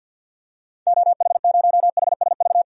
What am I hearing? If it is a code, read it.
Morse code, heard as a keyed tone, about 47 wpm, 700 Hz. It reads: OH0HSV
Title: DXCC-100 Call Signs - 6 Characters - Repeated 2x - 50wpm